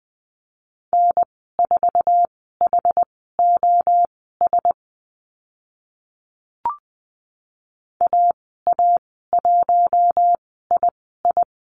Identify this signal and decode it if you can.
Morse code, heard as a keyed tone, about 20 wpm, spaced as for 14 wpm, 700 Hz. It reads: N4HOS AA1II
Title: 088 Lesson 15g - Number 5 - Call Signs 20wpm